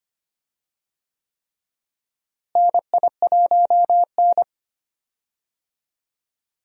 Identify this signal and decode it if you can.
Morse code, heard as a keyed tone, about 25 wpm, 700 Hz. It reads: NI1N
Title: USA Call Signs - 4 Characters - Repeated 3x - 25wpm